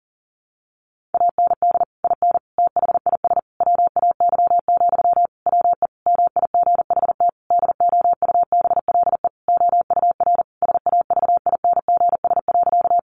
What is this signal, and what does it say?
Morse code, heard as a keyed tone, about 40 wpm, 700 Hz. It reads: AND IN THIS WAY, WE MIGHT DOUBLE OUR SAVINGS.
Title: Sentences from Top 1400 Words 40wpm